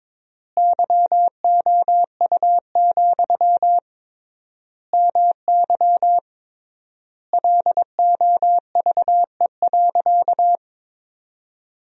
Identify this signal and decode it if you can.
Morse code, heard as a keyed tone, about 22 wpm, 700 Hz. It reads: YOU, MY LOVE.
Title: 3x Word-Spacing - Sentences from Top 400 Words - 22wpm